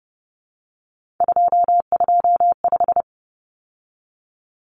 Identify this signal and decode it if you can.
Morse code, heard as a keyed tone, about 30 wpm, 700 Hz. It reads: 225
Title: Common Cut Numbers - 3-Digit 30wpm